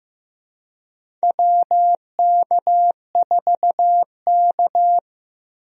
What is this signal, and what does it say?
Morse code, heard as a keyed tone, about 15 wpm, 700 Hz. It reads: WK4K
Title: USA Call Signs - 4 Characters - Repeated 3x - 15wpm